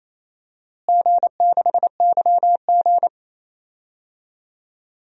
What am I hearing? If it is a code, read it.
Morse code, heard as a keyed tone, about 28 wpm, 700 Hz. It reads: G6YG
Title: DXCC-100 Call Signs - 4 Characters - Repeated 2x - 28wpm